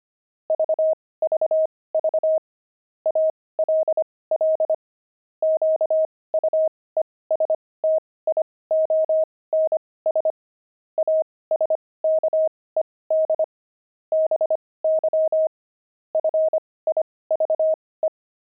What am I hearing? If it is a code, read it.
Morse code, heard as a keyed tone, about 25 wpm, spaced as for 18 wpm, 630 Hz. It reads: VVV ALL QUESTIONS ASKED BY FIVE
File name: pangram-fs-18-cs-25